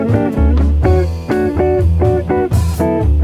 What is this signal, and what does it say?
There is no code.